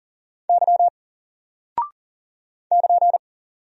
Morse code, heard as a keyed tone, about 39 wpm, 700 Hz. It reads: Y <KN>
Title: Single Character (Letter:Number:Punctuation:Prosign) 40wpm